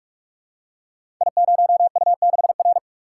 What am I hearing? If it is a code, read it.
Morse code, heard as a keyed tone, about 44 wpm, 700 Hz. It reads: I0U6R
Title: Code Groups - 5 Letter:Numbers - 45wpm